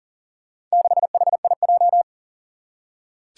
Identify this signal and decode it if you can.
Morse code, heard as a keyed tone, about 38 wpm, 700 Hz. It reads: 6HIJ
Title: Four Challenging Letter-Number Combinations 40wpm